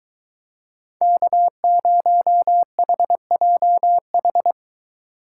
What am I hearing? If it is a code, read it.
Morse code, heard as a keyed tone, about 22 wpm, 700 Hz. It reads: K0HJH